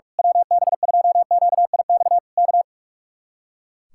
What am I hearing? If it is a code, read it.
Morse code, heard as a keyed tone, about 41 wpm, 700 Hz. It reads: WB2QIX K
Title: Speed-Racing Common QSO Phrases - 45wpm